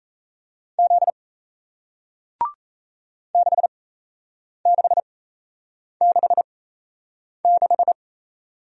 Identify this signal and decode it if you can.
Morse code, heard as a keyed tone, about 37 wpm, spaced as for 8 wpm, 700 Hz. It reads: Z 6666